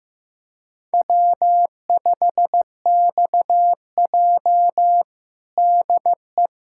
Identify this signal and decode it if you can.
Morse code, heard as a keyed tone, about 15 wpm, 700 Hz. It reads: W5XJ DE
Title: USA Callsign DE Callsign - 4 Characters - 15wpm